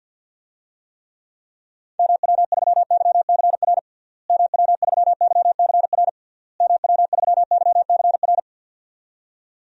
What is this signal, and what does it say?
Morse code, heard as a keyed tone, about 52 wpm, 700 Hz. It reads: MW3YCR MW3YCR MW3YCR
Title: DXCC-100 Call Signs - 6 Characters - Repeated 3x - 50wpm